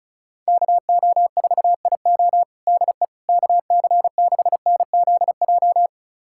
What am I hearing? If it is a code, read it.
Morse code, heard as a keyed tone, about 36 wpm, 700 Hz. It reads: KO4IO DE KC6NZJ
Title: USA Callsign DE Callsign 35wpm